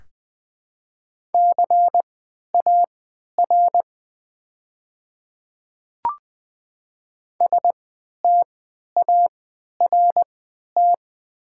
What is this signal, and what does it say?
Morse code, heard as a keyed tone, about 20 wpm, spaced as for 11 wpm, 700 Hz. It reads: CAR START
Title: 3x Character Spacing - Top 500 Words - 20wpm